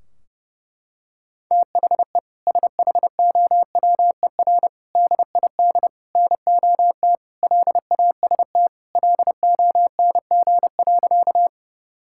Morse code, heard as a keyed tone, about 32 wpm, 700 Hz. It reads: THE SHOWER DID NOT LAST LONG.
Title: Sentences from Top 1600 Words 30wpm